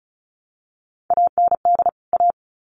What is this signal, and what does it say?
Morse code, heard as a keyed tone, about 34 wpm, 700 Hz. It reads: AND A
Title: Sets of 2 Words 35wpm